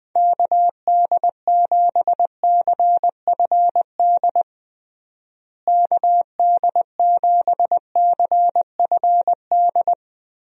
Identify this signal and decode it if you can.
Morse code, heard as a keyed tone, about 20 wpm, 700 Hz. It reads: KD7CFD KD7CFD
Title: USA Call Signs - 6 Characters - Repeated 3x - 20wpm